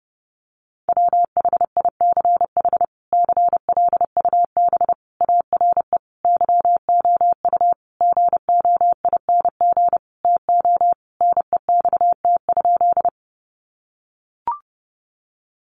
Morse code, heard as a keyed tone, about 30 wpm, 700 Hz. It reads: WHICH CLUB ARE YOU GOING TO NEXT?
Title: Sentences from Top 1200 Words 30wpm